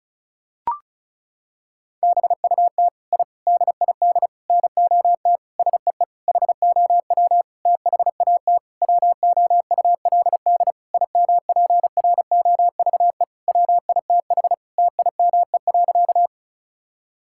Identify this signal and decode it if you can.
Morse code, heard as a keyed tone, about 34 wpm, 700 Hz. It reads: BUT I DID NOT SEE HOW THAT WOULD IMPROVE WITH TIME.
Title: Sentences from Top 1000 Words - Review Part 5 - 35wpm